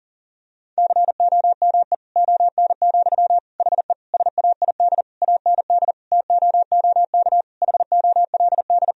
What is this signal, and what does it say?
Morse code, heard as a keyed tone, about 40 wpm, 700 Hz. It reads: COME ON, HE SAID AND TOOK HOLD
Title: Sentences from Top 1000 Words - Review Part 2 - 40wpm